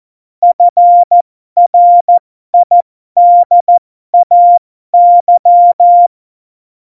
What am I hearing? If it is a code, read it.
Morse code, heard as a keyed tone, about 14 wpm, spaced as for 12 wpm, 700 Hz. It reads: FRIDAY
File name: Morsekurs / Witze